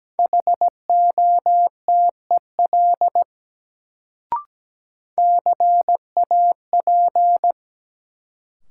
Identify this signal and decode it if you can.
Morse code, heard as a keyed tone, about 17 wpm, 700 Hz. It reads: HOTEL CAP
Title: Sentences from Top 1600 Words - New Word Drill Alt - 17wpm